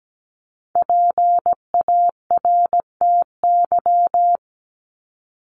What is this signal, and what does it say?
Morse code, heard as a keyed tone, about 17 wpm, 700 Hz. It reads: PARTY
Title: ES:S Suffix - Top 100 Words with Base Words 17wpm